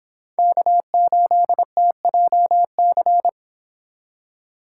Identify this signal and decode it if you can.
Morse code, heard as a keyed tone, about 26 wpm, 700 Hz. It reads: K8TJC
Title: Speed Racing - USA Call Signs 22wpm